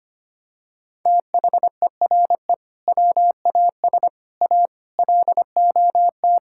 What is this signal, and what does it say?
Morse code, heard as a keyed tone, about 25 wpm, 700 Hz. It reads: THERE WAS A LOT
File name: practice / 25wpm / Sentences with 4-Word Phrases - 25wpm